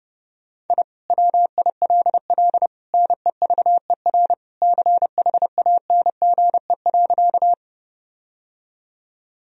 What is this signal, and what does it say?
Morse code, heard as a keyed tone, about 31 wpm, 700 Hz. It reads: I WILL NEVER CHANGE.
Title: Sentences from Top 1000 Words - Review Part 1 - 30wpm